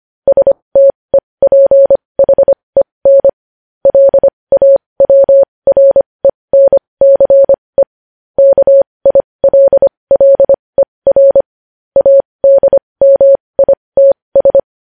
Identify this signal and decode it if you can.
Morse code, heard as a keyed tone, about 25 wpm, spaced as for 20 wpm, 550 Hz. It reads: STEPHEN LAWRENCE KILLER ADMITS
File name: podcast / News headlines-1x-25x20